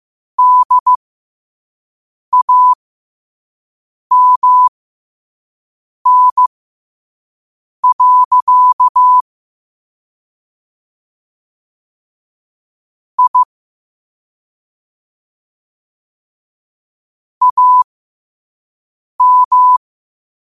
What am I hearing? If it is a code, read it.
Morse code, heard as a keyed tone, about 15 wpm, spaced as for 5 wpm, 1000 Hz. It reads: DAMN. I AM
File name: qotdfiles / 5wpm